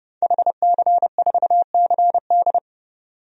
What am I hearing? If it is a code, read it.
Morse code, heard as a keyed tone, about 30 wpm, 700 Hz. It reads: HC4CD
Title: DXCC-100 Call Signs - 5 Characters - 30wpm